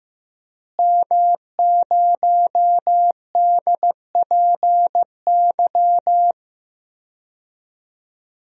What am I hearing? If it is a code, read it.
Morse code, heard as a keyed tone, about 16 wpm, 700 Hz. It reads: M0DPY
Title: DXCC-100 Call Signs - 5 Characters - Repeated 3x - 15wpm